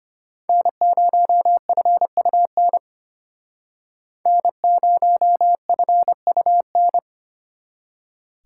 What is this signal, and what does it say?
Morse code, heard as a keyed tone, about 33 wpm, 700 Hz. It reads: N0FUN N0FUN